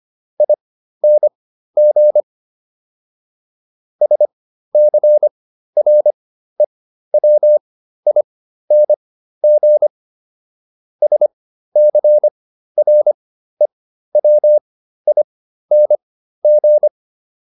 Morse code, heard as a keyed tone, about 25 wpm, spaced as for 13 wpm, 600 Hz. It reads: ING SCREWING SCREWING